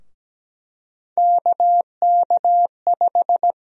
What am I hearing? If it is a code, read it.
Morse code, heard as a keyed tone, about 17 wpm, 700 Hz. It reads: KK5
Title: USA Call Sign Prefixes - Repeated 2x - 17wpm